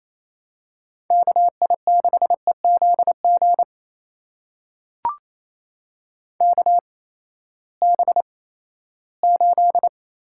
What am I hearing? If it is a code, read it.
Morse code, heard as a keyed tone, about 27 wpm, spaced as for 16 wpm, 700 Hz. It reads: KI6EZG KB8
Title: USA Call Signs - 6 Characters - 8x Character Spacing - 28wpm